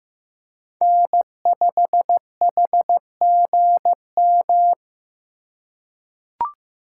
Morse code, heard as a keyed tone, about 15 wpm, 700 Hz. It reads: N5HGM
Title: USA Call Signs - 4x Character Spacing - 15wpm